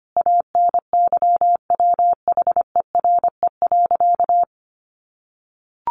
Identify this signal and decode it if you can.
Morse code, heard as a keyed tone, about 25 wpm, 700 Hz. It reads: ANYWHERE.
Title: Sentences from Top 1400 Words 25wpm